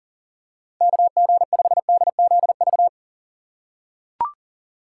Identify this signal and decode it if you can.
Morse code, heard as a keyed tone, about 37 wpm, 700 Hz. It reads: KG5DZV
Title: USA Call Signs - 6 Characters - 6x Character Spacing - 40wpm